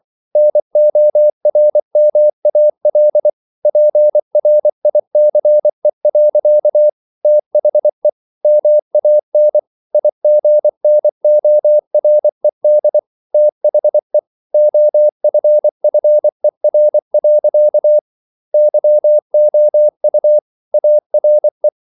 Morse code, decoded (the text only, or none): NORMAL PRICE. THE MAN IGNORED THE OFFER. YOU ARE